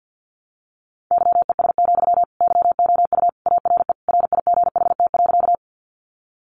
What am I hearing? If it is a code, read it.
Morse code, heard as a keyed tone, about 50 wpm, 700 Hz. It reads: YES, YOU ARE RIGHT.
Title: Sentences from Top 500 Words 50wpm